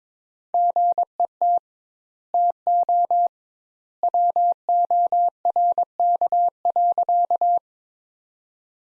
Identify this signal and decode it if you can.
Morse code, heard as a keyed tone, about 21 wpm, 700 Hz. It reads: GET TO WORK.
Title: Sentences Top 200 Words - 22wpm